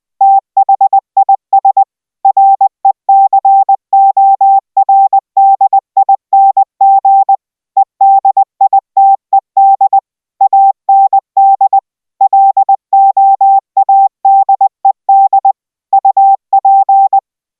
Morse code, heard as a keyed tone, about 20 wpm, 780 Hz. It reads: THIS RECORDING EDITED AND LOADED UP